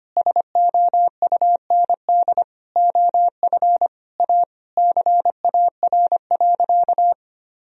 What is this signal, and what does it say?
Morse code, heard as a keyed tone, about 25 wpm, 700 Hz. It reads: SOUND OF A CAR.
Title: Sentences from Top 1000 Words - Review Part 1 - 25wpm